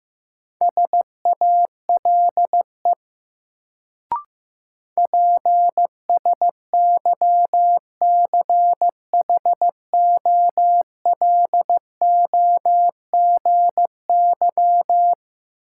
Morse code, heard as a keyed tone, about 16 wpm, 700 Hz. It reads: SALE PSYCHOLOGY
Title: Sentences from Top 1400 Words - New Word Drill Alt - 15wpm